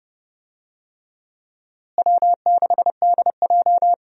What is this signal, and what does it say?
Morse code, heard as a keyed tone, about 28 wpm, 700 Hz. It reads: W6DJ
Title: USA Call Signs - 4 Characters - Repeated 3x - 30wpm